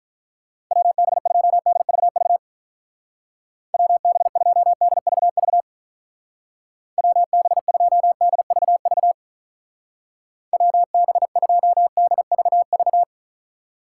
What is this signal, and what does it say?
Morse code, heard as a keyed tone, about 42 wpm, 700 Hz. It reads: WB2DVV WB2DVV WB2DVV WB2DVV
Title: Speed Racing - USA 2x3 Call Signs 35wpm